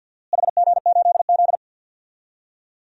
Morse code, heard as a keyed tone, about 46 wpm, 700 Hz. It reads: HG8Z
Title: DXCC-100 Call Signs - 4 Characters - Repeated 3x - 50wpm